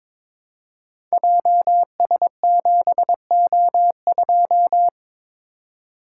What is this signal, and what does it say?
Morse code, heard as a keyed tone, about 22 wpm, 700 Hz. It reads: JS7O2
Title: Code Groups - 5 Letter:Numbers - 22wpm